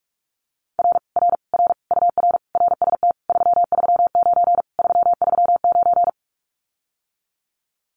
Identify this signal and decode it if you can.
Morse code, heard as a keyed tone, about 46 wpm, 700 Hz. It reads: R R R UR RST 339 339
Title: Common QSO Phrases Encore 45wpm